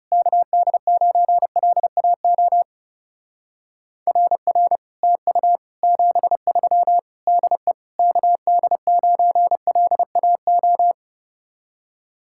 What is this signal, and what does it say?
Morse code, heard as a keyed tone, about 38 wpm, 700 Hz. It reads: KD9LAO RR TU 73 DE KD9LAO